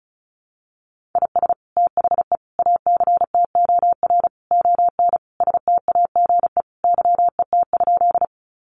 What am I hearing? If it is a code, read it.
Morse code, heard as a keyed tone, about 34 wpm, 700 Hz. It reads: IS THE ACTOR ON STAGE YET?
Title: Sentences from Top 1500 Words 35wpm